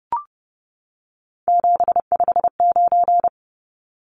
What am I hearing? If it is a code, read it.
Morse code, heard as a keyed tone, about 30 wpm, 700 Hz. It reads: 759
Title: Numbers - 3-Digit 30wpm